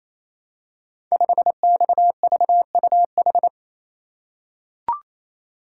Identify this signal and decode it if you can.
Morse code, heard as a keyed tone, about 27 wpm, 700 Hz. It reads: 5XVUH
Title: Code Groups - 5 Letter:Numbers - 28wpm